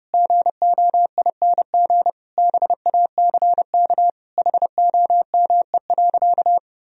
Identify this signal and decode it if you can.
Morse code, heard as a keyed tone, about 30 wpm, 700 Hz. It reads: GOING BACK HOME.
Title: Sentences from Top 500 Words and Abbreviations 30wpm